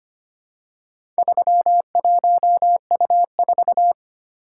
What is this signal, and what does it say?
Morse code, heard as a keyed tone, about 25 wpm, 700 Hz. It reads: 31U4
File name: practice / 25wpm / Four Letter-Number Combinations - 25wpm